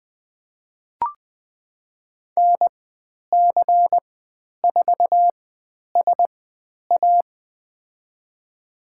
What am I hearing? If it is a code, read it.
Morse code, heard as a keyed tone, about 20 wpm, spaced as for 10 wpm, 700 Hz. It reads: NC4SA
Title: 095 Lesson 16g - Letter c - Call Signs 20wpm